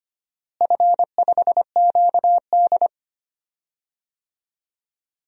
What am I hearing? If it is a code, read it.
Morse code, heard as a keyed tone, about 25 wpm, 700 Hz. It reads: F5QD
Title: DXCC-100 Call Signs - 4 Characters - Repeated 3x - 25wpm